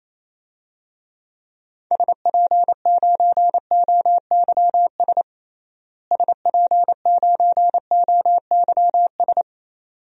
Morse code, heard as a keyed tone, about 28 wpm, 700 Hz. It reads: SP9OYS SP9OYS
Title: DXCC-100 Call Signs - 6 Characters - Repeated 2x - 28wpm